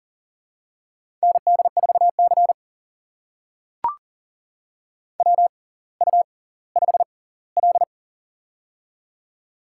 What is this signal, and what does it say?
Morse code, heard as a keyed tone, about 39 wpm, spaced as for 20 wpm, 690 Hz. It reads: ND4C WU5L
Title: USA Call Signs - 4 Characters - 6x Character Spacing - 40wpm